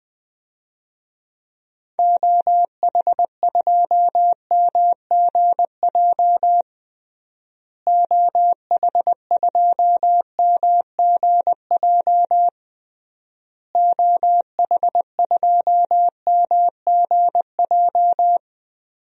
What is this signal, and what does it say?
Morse code, heard as a keyed tone, about 19 wpm, 700 Hz. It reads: OH2MGJ OH2MGJ OH2MGJ
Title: DXCC-100 Call Signs - 6 Characters - Repeated 3x - 20wpm